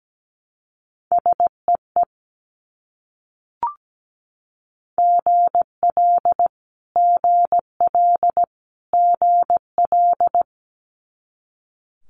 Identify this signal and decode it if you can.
Morse code, heard as a keyed tone, about 17 wpm, 690 Hz. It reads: SEE GL GL GL
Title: Multi-letter Phonemes Repeated 3x 17wpm